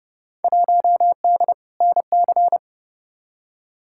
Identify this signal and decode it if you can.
Morse code, heard as a keyed tone, about 32 wpm, 700 Hz. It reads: 1D NC
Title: Field Day Exchanges Encore - 30wpm